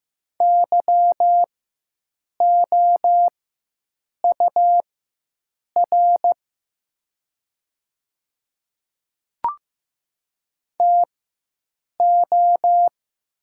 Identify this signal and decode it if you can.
Morse code, heard as a keyed tone, about 15 wpm, spaced as for 7 wpm, 700 Hz. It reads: YOUR TO
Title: 4x Character Spacing - Top 100 Words - 15wpm